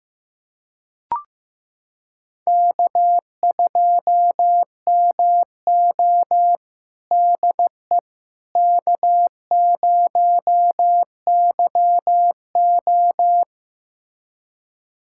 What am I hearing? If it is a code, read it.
Morse code, heard as a keyed tone, about 15 wpm, 700 Hz. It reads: K2MO DE K0YO
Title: USA Callsign DE Callsign - 4 Characters - 15wpm